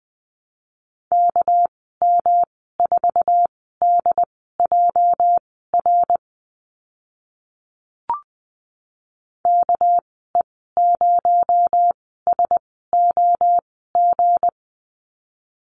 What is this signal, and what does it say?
Morse code, heard as a keyed tone, about 20 wpm, spaced as for 14 wpm, 700 Hz. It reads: KM4DJR KE0SOG
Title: 245 Lesson 39g - Letter z - Call Signs 20wpm